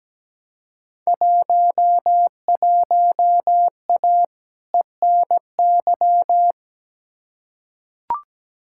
Morse code, heard as a keyed tone, about 17 wpm, 700 Hz. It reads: 11A ENY
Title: Field Day Exchanges Encore - 17wpm